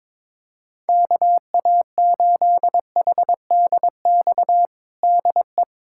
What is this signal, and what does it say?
Morse code, heard as a keyed tone, about 21 wpm, 700 Hz. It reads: KA8HDX DE
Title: USA Callsign DE Callsign - 6 Characters - 22wpm